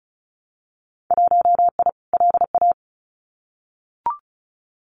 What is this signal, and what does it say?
Morse code, heard as a keyed tone, about 36 wpm, 700 Hz. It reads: 1I LA
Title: Winter Field Day Exchanges Encore 35wpm